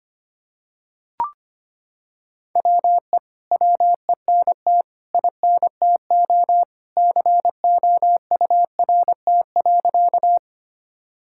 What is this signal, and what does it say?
Morse code, heard as a keyed tone, about 25 wpm, 700 Hz. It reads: WE WENT INTO COURT.
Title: Sentences from Top 1600 Words 25wpm